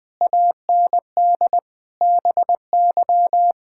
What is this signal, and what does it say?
Morse code, heard as a keyed tone, about 20 wpm, 700 Hz. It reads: AND BY
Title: Pangram Challenge v2 - 20wpm